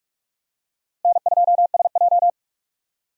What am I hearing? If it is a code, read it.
Morse code, heard as a keyed tone, about 46 wpm, 700 Hz. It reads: N2SJ
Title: USA Call Signs - 4 Characters - Repeated 2x - 45wpm